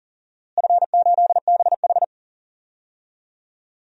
Morse code, heard as a keyed tone, about 40 wpm, 700 Hz. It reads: F8BH
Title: DXCC-100 Call Signs - 4 Characters - Repeated 3x - 40wpm